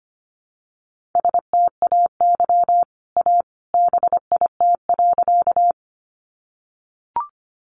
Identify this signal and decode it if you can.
Morse code, heard as a keyed tone, about 25 wpm, 700 Hz. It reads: STAY A BIT.
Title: Sentences from Top 900 Words 25wpm